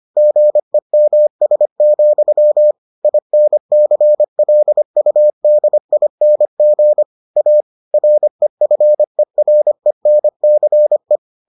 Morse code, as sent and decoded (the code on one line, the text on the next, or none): --. -- ... --..-- / .. -. -.-. .-.. ..- -.. .. -. --. / .- / .-. ..-. .-. -. -.-. .
GEMS, INCLUDING A REFERENCE